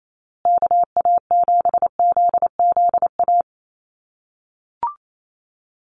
Morse code, heard as a keyed tone, about 28 wpm, 700 Hz. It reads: KA7ZZA
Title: USA Call Signs Repeated 3x 28wpm